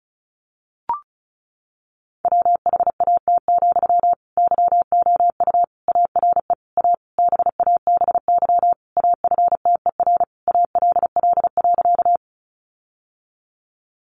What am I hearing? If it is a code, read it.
Morse code, heard as a keyed tone, about 34 wpm, 700 Hz. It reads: WHAT, YOU ARE A BABY AFTER ALL.
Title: Sentences from Top 1300 Words 35wpm